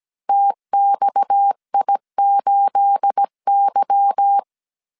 Morse code, heard as a keyed tone, about 17 wpm, 800 Hz. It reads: TXI8Y
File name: CW / DET / 1 9 1500C S C 16wpm 2909